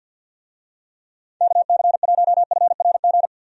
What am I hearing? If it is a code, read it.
Morse code, heard as a keyed tone, about 47 wpm, 700 Hz. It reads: KC1FRG
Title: USA Call Signs - 6 Characters - 50wpm